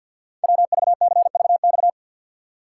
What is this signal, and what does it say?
Morse code, heard as a keyed tone, about 50 wpm, 700 Hz. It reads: WVCVX
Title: Code Groups - 5 Letter:Numbers - 50wpm